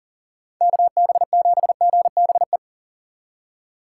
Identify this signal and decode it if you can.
Morse code, heard as a keyed tone, about 40 wpm, 700 Hz. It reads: KB7GBE
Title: USA Call Signs - 6 Characters - 2x Character Spacing - 40wpm